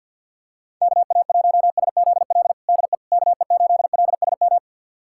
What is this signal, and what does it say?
Morse code, heard as a keyed tone, about 49 wpm, 700 Hz. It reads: KA1SZL DE KE8LSM
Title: USA Callsign DE Callsign - 6 Characters - 50wpm